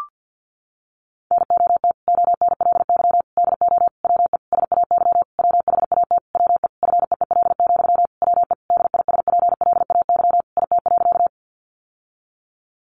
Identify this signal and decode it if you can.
Morse code, heard as a keyed tone, about 50 wpm, 700 Hz. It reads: NOT ONLY DO WE SAY WHAT WE FEEL, WE DISPLAY IT.
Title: Sentences from Top 1200 Words 50wpm